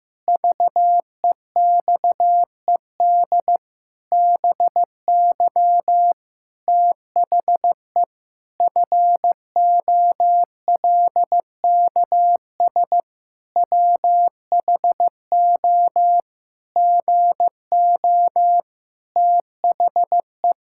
Morse code, heard as a keyed tone, about 15 wpm, 700 Hz. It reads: VEXED BY THE FOLKS WHO GO THE
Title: Pangram Challenge v2 - 15wpm